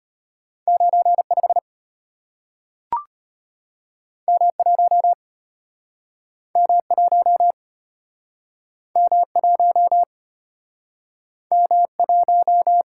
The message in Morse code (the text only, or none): 95 M1 M1 M1 M1